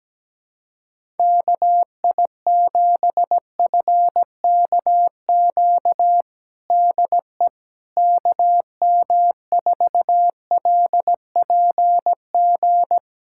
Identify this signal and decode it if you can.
Morse code, heard as a keyed tone, about 17 wpm, 700 Hz. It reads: KI7FKQ DE KM4LPG
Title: USA Callsign DE Callsign - 6 Characters - 17wpm